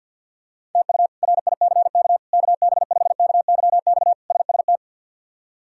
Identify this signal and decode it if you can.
Morse code, heard as a keyed tone, about 52 wpm, 700 Hz. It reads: TU RICK KB5KYX SST
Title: K1USN SST QSO Practice - 50wpm